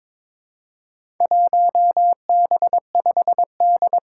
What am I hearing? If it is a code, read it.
Morse code, heard as a keyed tone, about 22 wpm, 700 Hz. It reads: 1B5D
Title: Four Challenging Letter-Number Combinations 22wpm